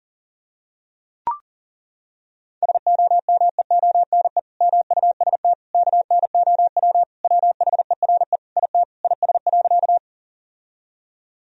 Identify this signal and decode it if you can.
Morse code, heard as a keyed tone, about 37 wpm, 700 Hz. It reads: SOMEONE MUST KNOW WHERE IT IS.
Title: Sentences from Top 1000 Words - Review Part 4 - 40wpm